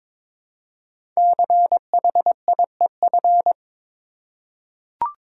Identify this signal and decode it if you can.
Morse code, heard as a keyed tone, about 22 wpm, 700 Hz. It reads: CHIEF
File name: 5 Letter Words - First 3 Letters Given - 22wpm